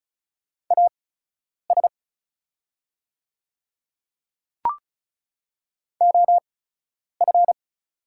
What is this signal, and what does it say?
Morse code, heard as a keyed tone, about 36 wpm, spaced as for 10 wpm, 700 Hz. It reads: AS OF
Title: 8x Character Spacing - Top 300 Words - 35wpm